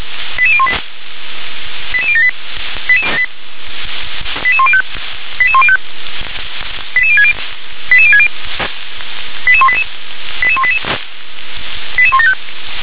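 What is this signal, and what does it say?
Five-tone selective call (ZVEI), heard as quick runs of five tones, tones of about 70 ms. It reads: 88018 88078 88105 88105 88060 88060 88188 88188 88175
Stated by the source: ZVEI five-tone selective call received on an SDR.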